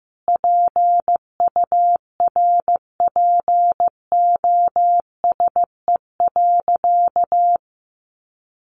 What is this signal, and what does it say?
Morse code, heard as a keyed tone, about 15 wpm, 700 Hz. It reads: PURPOSE.
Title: Sentences from Top 800 Words 15wpm